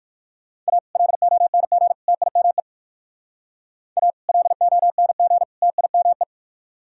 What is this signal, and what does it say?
Morse code, heard as a keyed tone, about 57 wpm, 700 Hz. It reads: A LONG TIME A LONG TIME